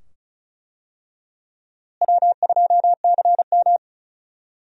Morse code, heard as a keyed tone, about 34 wpm, 700 Hz. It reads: W2CM
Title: USA Call Signs - 4 Characters - Repeated 2x - 35wpm